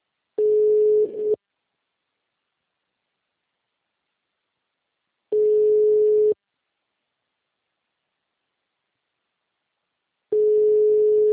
Ringback tone (Europe, 425 Hz).